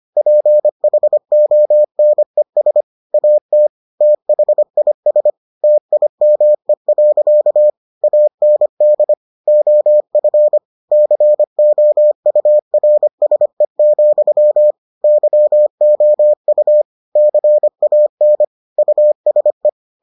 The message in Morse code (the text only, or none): PHONES AT THIS TIME. AND OF COURSE, YOU CAN USE